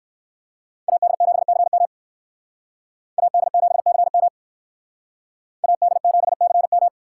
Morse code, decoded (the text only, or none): AD7CM AD7CM AD7CM